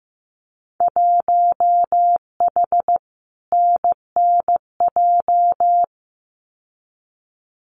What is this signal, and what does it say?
Morse code, heard as a keyed tone, about 15 wpm, 700 Hz. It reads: 1H NNJ
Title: Winter Field Day Exchanges 15wpm